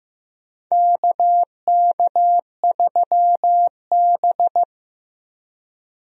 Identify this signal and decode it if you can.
Morse code, heard as a keyed tone, about 16 wpm, 700 Hz. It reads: KK3B
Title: USA Call Signs - 4 Characters - 15wpm